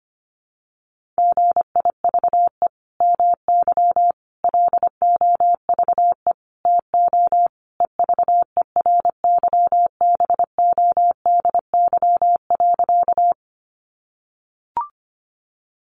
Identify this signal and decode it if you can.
Morse code, heard as a keyed tone, about 25 wpm, 700 Hz. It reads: GIVE MY LOVE TO EVERYBODY.
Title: Sentences from Top 1400 Words 25wpm